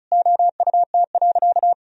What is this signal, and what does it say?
Morse code, heard as a keyed tone, about 36 wpm, 700 Hz. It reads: OUT.
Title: Sentences from Top 1000 Words - Review Part 1 - 35wpm